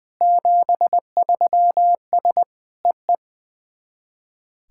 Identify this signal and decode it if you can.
Morse code, heard as a keyed tone, about 20 wpm, 690 Hz. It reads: 73S EE
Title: SOTA QSO Practice 20wpm